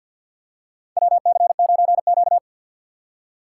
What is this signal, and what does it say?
Morse code, heard as a keyed tone, about 51 wpm, 700 Hz. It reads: WC9Q
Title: USA Call Signs - 4 Characters - 3x Character Spacing - 50wpm